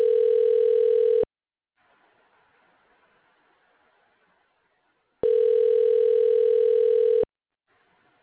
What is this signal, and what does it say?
Ringback tone (North America), 440 + 480 Hz.
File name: recordings / RE951f74872dc1a5813ec26a50115e4931